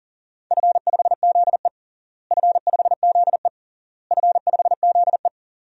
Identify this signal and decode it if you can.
Morse code, heard as a keyed tone, about 37 wpm, 700 Hz. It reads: F5ZE F5ZE F5ZE
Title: DXCC-100 Call Signs - 4 Characters - Repeated 3x - 40wpm